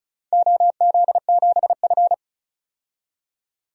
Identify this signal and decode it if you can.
Morse code, heard as a keyed tone, about 34 wpm, 700 Hz. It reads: OZ7F
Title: DXCC-100 Call Signs - 4 Characters - 35wpm